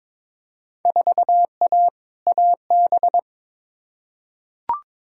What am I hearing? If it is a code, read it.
Morse code, heard as a keyed tone, about 22 wpm, 700 Hz. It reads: 4A AB
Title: Field Day Exchanges Encore - 22wpm